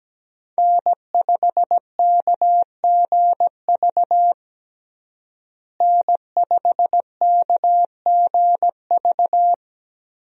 Morse code, heard as a keyed tone, about 17 wpm, 700 Hz. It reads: N5KGV N5KGV
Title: USA Call Signs - 5 Characters - Repeated 2x - 17wpm